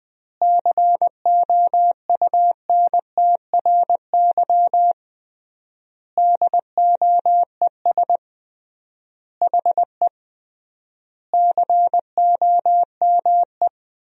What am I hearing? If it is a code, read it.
Morse code, heard as a keyed tone, about 20 wpm, 700 Hz. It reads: COUNTRY DOES HE COME
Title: 3x Word-Spacing - Sentences from Top 300 Words - 20wpm